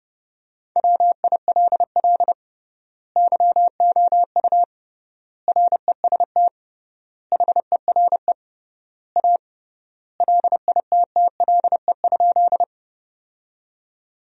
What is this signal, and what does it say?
Morse code, heard as a keyed tone, about 30 wpm, 700 Hz. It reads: WILL YOU REST HERE A LITTLE?
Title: 3x Word-Spacing - Sentences from Top 500 Words - 30wpm